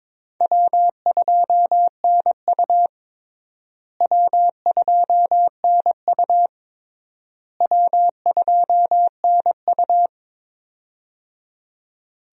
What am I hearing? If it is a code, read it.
Morse code, heard as a keyed tone, about 22 wpm, 700 Hz. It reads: W2NU W2NU W2NU
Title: USA Call Signs - 4 Characters - Repeated 3x - 22wpm